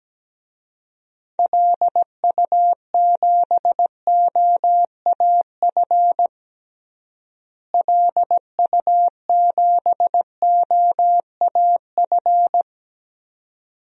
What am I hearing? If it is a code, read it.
Morse code, heard as a keyed tone, about 17 wpm, 700 Hz. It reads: LU7OAF LU7OAF
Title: DXCC-100 Call Signs - 6 Characters - Repeated 2x - 17wpm